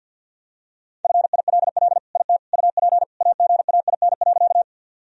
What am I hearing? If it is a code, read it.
Morse code, heard as a keyed tone, about 50 wpm, 700 Hz. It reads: FILL IT UP AGAIN.
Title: Sentences from Top 1000 Words - Review Part 3 - 50wpm